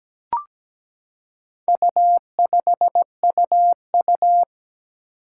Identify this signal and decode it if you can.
Morse code, heard as a keyed tone, about 17 wpm, 700 Hz. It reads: U5UU
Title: Four Challenging Letter-Number Combinations 17wpm